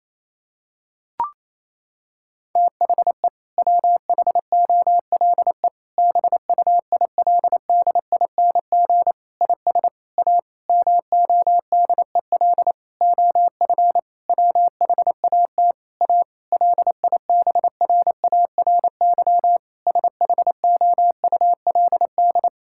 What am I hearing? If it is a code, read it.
Morse code, heard as a keyed tone, about 27 wpm, 700 Hz. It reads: THE WHOLE BUILDING IS A MODEL OF WHAT A LIBRARY SHOULD
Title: Sentences from Top 1000 Words - Review Part 5 - 28wpm